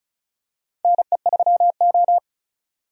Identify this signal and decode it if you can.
Morse code, heard as a keyed tone, about 36 wpm, 700 Hz. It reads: NE3O
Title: USA Call Signs - 4 Characters - 35wpm